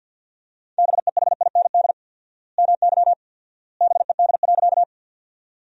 Morse code, heard as a keyed tone, about 50 wpm, 700 Hz. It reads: BEHIND MY BED.
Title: 4x Word-Spacing - Sentences from Top 500 Words - 50wpm